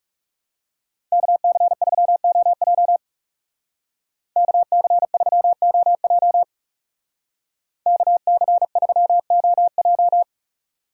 Morse code, heard as a keyed tone, about 38 wpm, 700 Hz. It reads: KC3OJ KC3OJ KC3OJ